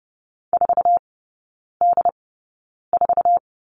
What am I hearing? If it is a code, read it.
Morse code, heard as a keyed tone, about 30 wpm, spaced as for 9 wpm, 700 Hz. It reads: 4D4
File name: Lesson 18e - Letter m - Three Characters 30wpm